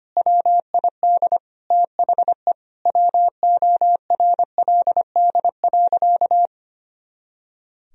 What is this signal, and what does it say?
Morse code, heard as a keyed tone, about 25 wpm, 700 Hz. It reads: WID THE WORLD.
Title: Sentences from Top 200 Words and Abbreviations 25wpm